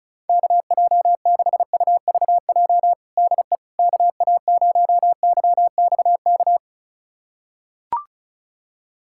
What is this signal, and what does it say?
Morse code, heard as a keyed tone, about 36 wpm, 700 Hz. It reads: KJ6UVJ DE KA0YXK
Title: USA Callsign DE Callsign - 6 Characters - 35wpm